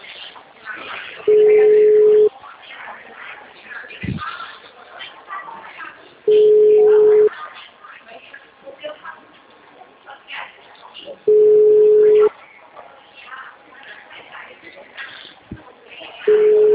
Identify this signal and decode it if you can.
Ringback tone (Europe, 425 Hz), with speech.
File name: out- -2011- - - .379